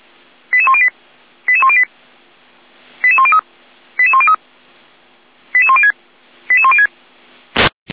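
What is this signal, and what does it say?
Five-tone selective call (ZVEI), heard as quick runs of five tones, tones of about 80 ms. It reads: 88108 88108 88103 88103 88106 88106
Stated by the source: ZVEI five-tone selective call, code 88229, received with an SDR.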